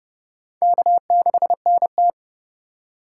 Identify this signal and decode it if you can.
Morse code, heard as a keyed tone, about 30 wpm, 700 Hz. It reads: K6NT
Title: USA Call Signs - 4 Characters - 4x Character Spacing - 30wpm